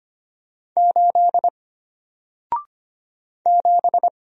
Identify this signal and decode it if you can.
Morse code, heard as a keyed tone, about 25 wpm, 700 Hz. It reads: 8 7
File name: Single-Letter-Number-25wpm